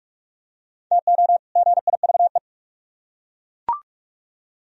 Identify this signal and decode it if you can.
Morse code, heard as a keyed tone, about 39 wpm, 700 Hz. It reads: TO GIVE